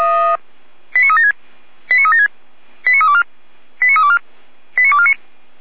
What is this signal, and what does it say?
Five-tone selective call (ZVEI), heard as quick runs of five tones, tones of about 70 ms. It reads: 89376 89376 89324 89324 89249
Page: /455/SDR/ZVEI/89250/